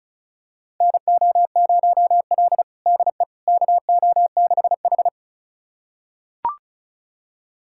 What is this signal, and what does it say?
Morse code, heard as a keyed tone, about 36 wpm, 700 Hz. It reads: NO0L DE KO6H
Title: USA Callsign DE Callsign - 4 Characters - 35wpm